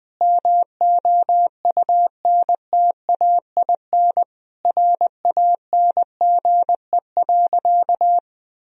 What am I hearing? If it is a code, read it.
Morse code, heard as a keyed tone, about 20 wpm, 700 Hz. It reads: MOUNTAIN RANGE.
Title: Sentences from Top 1000 Words - Review Part 4 - 20wpm